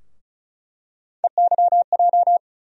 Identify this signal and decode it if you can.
Morse code, heard as a keyed tone, about 34 wpm, 700 Hz. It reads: EYJ
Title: USA Call Sign Postfixes - 3 Letters 35wpm